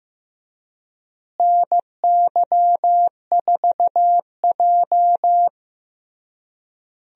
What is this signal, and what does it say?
Morse code, heard as a keyed tone, about 15 wpm, 700 Hz. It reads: NY4J
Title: USA Call Signs - 4 Characters - Repeated 3x - 15wpm